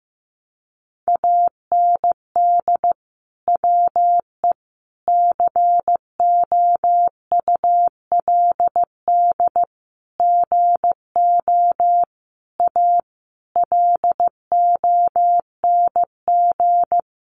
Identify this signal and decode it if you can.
Morse code, heard as a keyed tone, about 15 wpm, 700 Hz. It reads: AND WE COULD GO A LONG
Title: Sentences from Top 1200 Words 15wpm